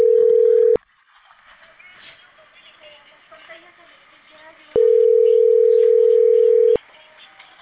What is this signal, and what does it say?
Ringback tone (North America), 440 + 480 Hz.